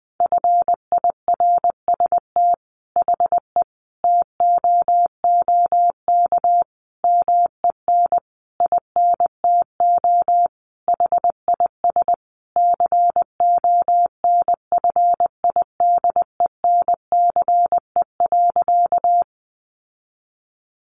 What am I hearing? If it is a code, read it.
Morse code, heard as a keyed tone, about 19 wpm, 700 Hz. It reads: FIRST HE TOOK MEN INTO HIS CONFIDENCE.
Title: Sentences from Top 1500 Words 20wpm